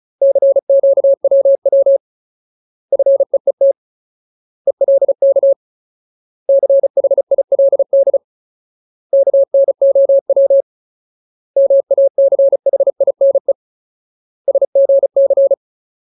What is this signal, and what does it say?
Morse code, heard as a keyed tone, about 34 wpm, 550 Hz. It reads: CQWW FEET ELK CHILD KNOW MACHINE SGC
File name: Ultimate Word List / 35WPM Ultimate